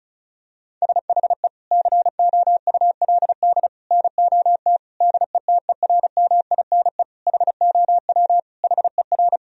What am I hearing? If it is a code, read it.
Morse code, heard as a keyed tone, about 34 wpm, 700 Hz. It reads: SHE COULD NOT DETERMINE HOW HER
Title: Sentences from Top 1000 Words - Review Part 4 - 35wpm